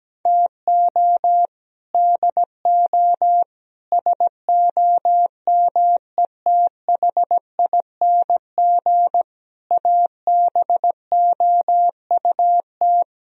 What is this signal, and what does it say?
Morse code, heard as a keyed tone, about 17 wpm, 700 Hz. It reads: TO DO SOMETHING ABOUT
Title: Speed-Racing - Sets of 5 Words 17wpm